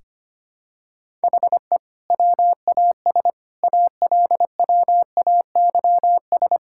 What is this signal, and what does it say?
Morse code, heard as a keyed tone, about 25 wpm, 700 Hz. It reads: HE WAS ALWAYS
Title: Sentences from Top 1000 Words - Review Part 6 - 25wpm